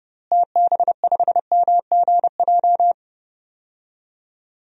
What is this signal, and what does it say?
Morse code, heard as a keyed tone, about 32 wpm, 700 Hz. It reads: TB5MGJ
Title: DXCC-100 Call Signs - 6 Characters - Repeated 3x - 30wpm